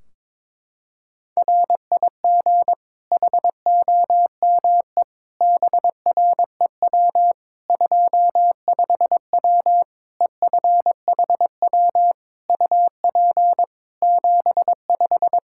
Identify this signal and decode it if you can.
Morse code, heard as a keyed tone, about 22 wpm, 700 Hz. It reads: RIG HOME BREW 25W EFHW UP 75
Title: K1USN SST QSO Practice 22wpm